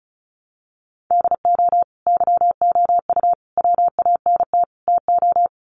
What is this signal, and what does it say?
Morse code, heard as a keyed tone, about 34 wpm, 700 Hz. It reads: DO YOU WANT TO
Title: Sets of 4 Words 35wpm